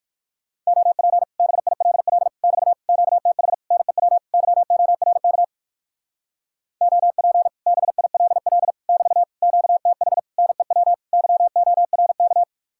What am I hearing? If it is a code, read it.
Morse code, heard as a keyed tone, about 57 wpm, 700 Hz. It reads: OP BILL <BT> QTH NEW YORK OP BILL <BT> QTH NEW YORK